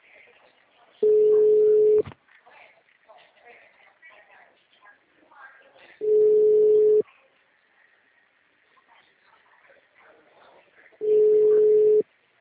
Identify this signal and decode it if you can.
Ringback tone (Europe, 425 Hz), with speech.